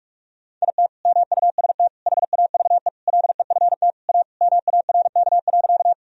Morse code, heard as a keyed tone, about 41 wpm, 700 Hz. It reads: IT MUST HAVE LEFT A MARK.
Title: Sentences from Top 1000 Words - Review Part 1 - 45wpm